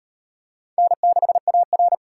Morse code, heard as a keyed tone, about 34 wpm, 700 Hz. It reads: N6AR